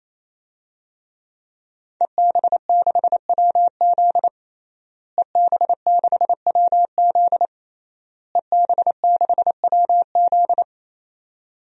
Morse code, heard as a keyed tone, about 28 wpm, 690 Hz. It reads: EB6WZ EB6WZ EB6WZ
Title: DXCC-100 Call Signs - 5 Characters - Repeated 3x - 28wpm